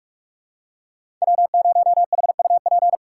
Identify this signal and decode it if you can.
Morse code, heard as a keyed tone, about 45 wpm, 700 Hz. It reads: W0HUP
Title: USA Call Signs - 5 Characters - 3x Character Spacing - 45wpm